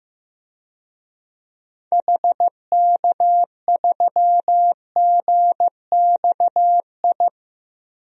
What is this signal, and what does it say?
Morse code, heard as a keyed tone, about 15 wpm, 700 Hz. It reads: HK3GXI
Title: DXCC-100 Call Signs - 6 Characters - 15wpm